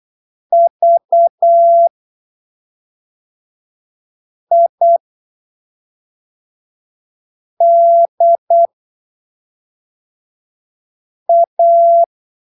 Morse code, heard as a keyed tone, about 8 wpm, spaced as for 3 wpm, 680 Hz. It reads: VIDA
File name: arquivos / texto / texto 28 08ppm